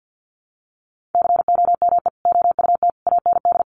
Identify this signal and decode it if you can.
Morse code, heard as a keyed tone, about 50 wpm, 700 Hz. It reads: COME OUT AND
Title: Sentences from Top 1300 Words 50wpm